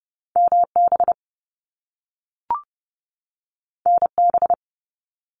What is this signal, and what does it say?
Morse code, heard as a keyed tone, about 30 wpm, 700 Hz. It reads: MB NB
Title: Canadian Provinces and Territories 30wpm